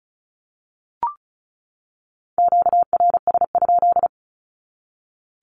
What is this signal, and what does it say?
Morse code, heard as a keyed tone, about 34 wpm, 700 Hz. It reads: QRS?
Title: Q-Codes 35wpm